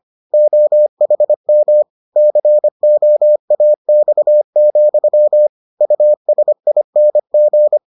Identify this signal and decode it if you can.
Morse code, heard as a keyed tone, about 25 wpm, 600 Hz. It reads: OHM COAX, USING